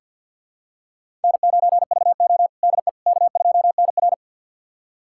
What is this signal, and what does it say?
Morse code, heard as a keyed tone, about 48 wpm, 700 Hz. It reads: N9VO DE K2NR
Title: USA Callsign DE Callsign - 4 Characters - 50wpm